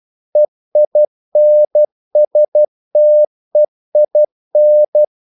Morse code, heard as a keyed tone, about 12 wpm, 600 Hz. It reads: EINSTEIN
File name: CW / Practice / EnglishQuotations0112